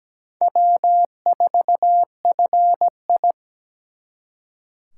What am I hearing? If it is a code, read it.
Morse code, heard as a keyed tone, about 20 wpm, 700 Hz. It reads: W4FI
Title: Speed Racing - USA 1x2 Call Signs 17wpm